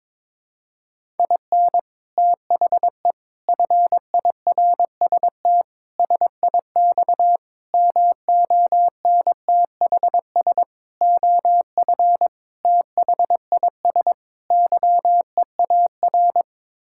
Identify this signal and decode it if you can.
Morse code, heard as a keyed tone, about 22 wpm, 700 Hz. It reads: IN THE FIRST SIX MONTHS OF THIS YEAR
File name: practice / 22wpm / Sets of 8 Words - 22wpm